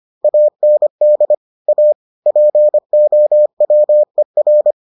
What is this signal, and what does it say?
Morse code, heard as a keyed tone, about 25 wpm, 600 Hz. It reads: AND A POWER